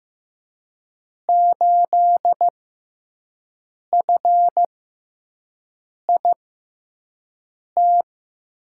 Morse code, heard as a keyed tone, about 15 wpm, spaced as for 5 wpm, 700 Hz. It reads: 8FIT
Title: USA Call Signs - 6x Character Spacing - 15wpm